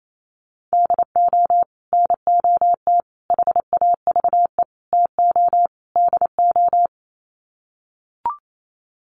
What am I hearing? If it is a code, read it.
Morse code, heard as a keyed tone, about 28 wpm, 700 Hz. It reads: DO NOT HAVE TO DO
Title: Sets of 5 Words 28wpm